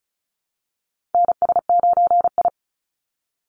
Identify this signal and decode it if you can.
Morse code, heard as a keyed tone, about 34 wpm, 700 Hz. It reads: NS9I
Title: Call Signs Repeated 2x - 35wpm